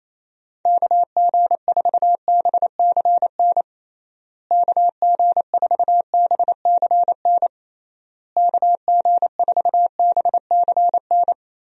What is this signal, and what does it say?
Morse code, heard as a keyed tone, about 28 wpm, 700 Hz. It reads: KG4BCN KG4BCN KG4BCN
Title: USA Call Signs - 6 Characters - Repeated 3x - 28wpm